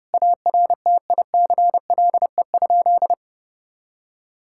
Morse code, heard as a keyed tone, about 32 wpm, 700 Hz. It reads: ARTICLE?
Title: Sentences from Top 1000 Words - Review Part 5 - 30wpm